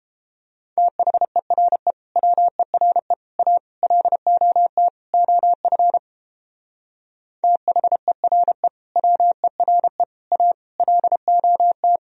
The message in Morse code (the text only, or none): THERE WERE A LOT OF THERE WERE A LOT